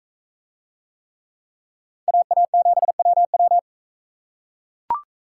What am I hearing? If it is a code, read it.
Morse code, heard as a keyed tone, about 40 wpm, 700 Hz. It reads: AA7WW